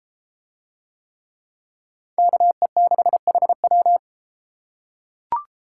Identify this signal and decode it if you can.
Morse code, heard as a keyed tone, about 33 wpm, 700 Hz. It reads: KE6HW